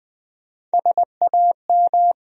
Morse code, heard as a keyed tone, about 20 wpm, 700 Hz. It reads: SAM
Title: USA Common Names - Max 4 Letters - 20wpm